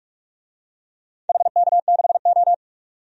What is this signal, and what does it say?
Morse code, heard as a keyed tone, about 43 wpm, 700 Hz. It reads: HK6O
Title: DXCC-100 Call Signs - 4 Characters - 45wpm